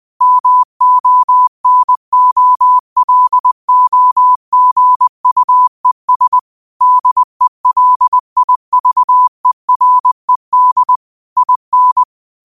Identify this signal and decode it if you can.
Morse code, heard as a keyed tone, about 20 wpm, 1000 Hz. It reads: MONOLOGUES DELIVERED IN